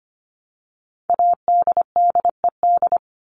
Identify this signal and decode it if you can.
Morse code, heard as a keyed tone, about 25 wpm, 700 Hz. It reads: ADDED
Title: ED Suffix - Top 100 Words with Base Words 25wpm